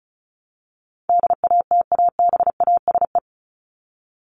Morse code, heard as a keyed tone, about 34 wpm, 700 Hz. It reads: DATABASE
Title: Top 1751-2000 Words 35wpm